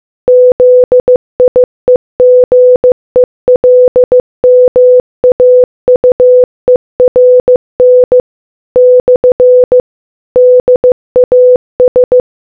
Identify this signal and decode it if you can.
Morse code, heard as a keyed tone, about 15 wpm, 500 Hz. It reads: ZIEGELMAUERN / DAS